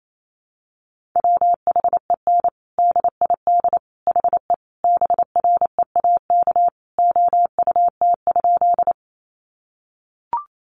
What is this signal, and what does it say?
Morse code, heard as a keyed tone, about 28 wpm, 700 Hz. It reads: WHEN DID HE BREAK OUT?
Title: Sentences from Top 900 Words 28wpm